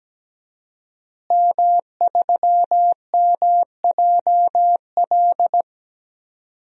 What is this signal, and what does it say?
Morse code, heard as a keyed tone, about 17 wpm, 700 Hz. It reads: M3MJL
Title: DXCC-100 Call Signs - 5 Characters - Repeated 2x - 17wpm